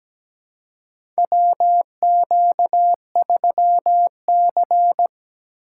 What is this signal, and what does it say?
Morse code, heard as a keyed tone, about 17 wpm, 700 Hz. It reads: WQ3C
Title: USA Call Signs - 4 Characters - 3x Character Spacing - 17wpm